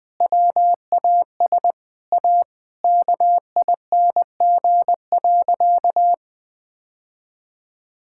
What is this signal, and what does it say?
Morse code, heard as a keyed tone, about 20 wpm, 700 Hz. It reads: WAS A KING.
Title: Sentences with 7-Word Phrases - 20wpm